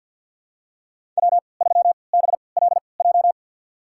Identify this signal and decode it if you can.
Morse code, heard as a keyed tone, about 47 wpm, spaced as for 28 wpm, 700 Hz. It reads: W3BLJ
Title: USA Call Signs - 3x Character Spacing - 50wpm